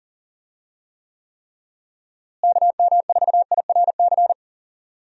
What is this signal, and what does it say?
Morse code, heard as a keyed tone, about 40 wpm, 700 Hz. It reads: KM4IRC
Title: USA Call Signs - 6 Characters - Repeated 3x - 40wpm